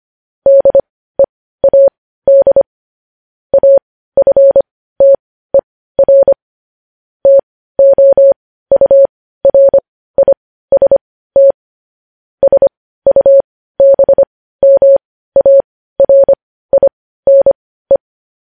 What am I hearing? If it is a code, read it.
Morse code, heard as a keyed tone, about 25 wpm, spaced as for 15 wpm, 550 Hz. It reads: DEAD AFTER TOURIST SUBMARINE
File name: podcast / News headlines-1x-25x15